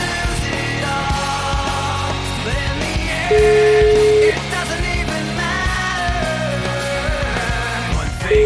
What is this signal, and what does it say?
Ringback tone (Europe, 425 Hz), with noise.